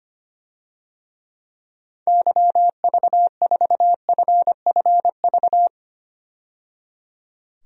Morse code, heard as a keyed tone, about 25 wpm, 700 Hz. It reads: YV4FFV
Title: DXCC-100 Call Signs - 6 Characters - 25wpm